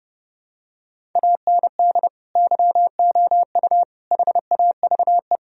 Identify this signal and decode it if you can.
Morse code, heard as a keyed tone, about 32 wpm, 700 Hz. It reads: AND YOU HAVE
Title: Sentences from Top 1000 Words - Review Part 1 - 30wpm